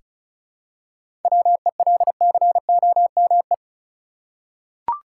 Morse code, heard as a keyed tone, about 36 wpm, 700 Hz. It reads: WELCOME
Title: Seven Letter Words - 35wpm